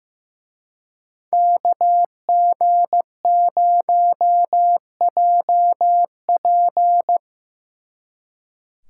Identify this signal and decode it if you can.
Morse code, heard as a keyed tone, about 18 wpm, 700 Hz. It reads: KG0JP
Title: Speed Racing - USA 2x2 Call Signs 15wpm